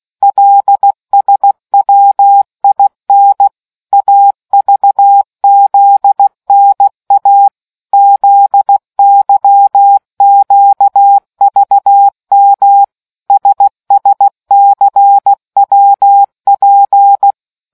Morse code, heard as a keyed tone, about 16 wpm, 800 Hz. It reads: LSWIN AVZNA ZYQVM SSCWP